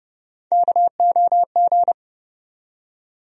Morse code, heard as a keyed tone, about 30 wpm, 700 Hz. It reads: KOG
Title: USA Call Sign Postfixes - 3 Letters 30wpm